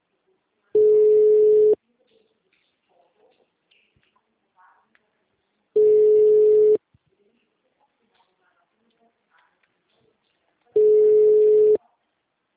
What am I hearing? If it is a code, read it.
Ringback tone (Europe, 425 Hz).